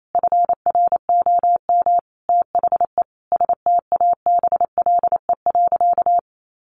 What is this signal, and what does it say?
Morse code, heard as a keyed tone, about 28 wpm, 700 Hz. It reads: FROM THE STABLE.
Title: Sentences from Top 1500 Words 28wpm